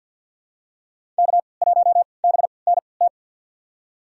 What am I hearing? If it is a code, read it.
Morse code, heard as a keyed tone, about 46 wpm, spaced as for 27 wpm, 700 Hz. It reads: K1BNT
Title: USA Call Signs - 5 Characters - 3x Character Spacing - 50wpm